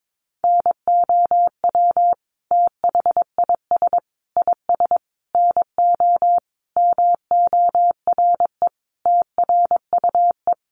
Morse code, heard as a keyed tone, about 22 wpm, 700 Hz. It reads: NOW THIS IS NO MORE TRUE
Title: Sentences from Top 1400 Words 22wpm